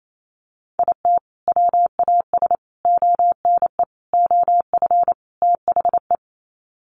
Morse code, heard as a keyed tone, about 28 wpm, 700 Hz. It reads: IT WAS ONE OF THE
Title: Sets of 5 Words 28wpm